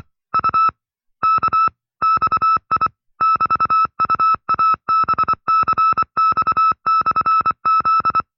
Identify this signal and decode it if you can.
Morse code, heard as a keyed tone, about 24 wpm, 1310 Hz. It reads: U K XI <BT>UABCX/Z
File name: FreeFiles / CW-practice / 20WPM-side2